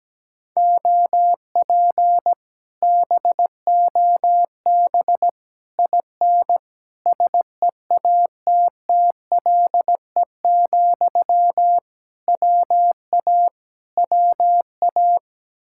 Morse code, heard as a keyed tone, about 17 wpm, 700 Hz. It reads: OP BOB IN SEATTLE, WA WA
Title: Speed-Racing Common QSO Phrases - 17wpm